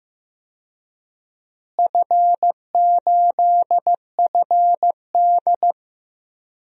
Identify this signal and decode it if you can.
Morse code, heard as a keyed tone, about 15 wpm, 700 Hz. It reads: F8FD
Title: DXCC-100 Call Signs - 4 Characters - Repeated 3x - 15wpm